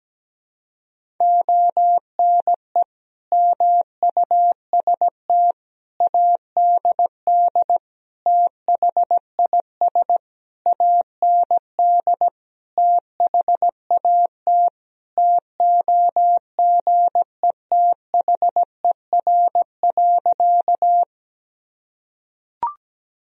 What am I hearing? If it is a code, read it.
Morse code, heard as a keyed tone, about 17 wpm, 700 Hz. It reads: ONE MUST ADD THIS AND THAT TOGETHER.
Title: Sentences from Top 1000 Words - Review Part 1 - 17wpm